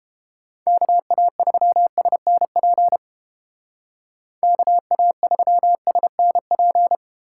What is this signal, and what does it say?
Morse code, heard as a keyed tone, about 31 wpm, 700 Hz. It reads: KA3SNP KA3SNP